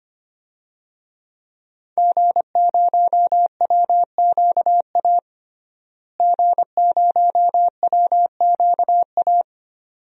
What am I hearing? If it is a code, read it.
Morse code, heard as a keyed tone, about 25 wpm, 700 Hz. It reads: G0WQA G0WQA
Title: DXCC-100 Call Signs - 5 Characters - Repeated 2x - 25wpm